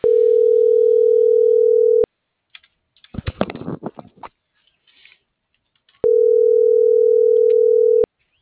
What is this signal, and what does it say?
Ringback tone (North America), 440 + 480 Hz.